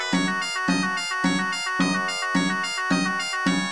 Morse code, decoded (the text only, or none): none